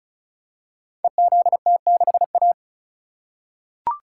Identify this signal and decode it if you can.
Morse code, heard as a keyed tone, about 35 wpm, 700 Hz. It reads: EZT6A
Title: Code Groups - 5 Letter:Numbers - 35wpm